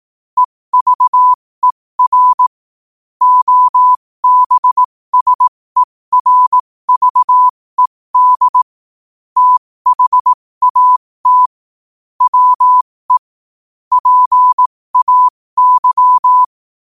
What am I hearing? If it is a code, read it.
Morse code, heard as a keyed tone, about 18 wpm, spaced as for 15 wpm, 1000 Hz. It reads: EVER OBSERVED THAT WE PAY
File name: qotdfiles / 15wpm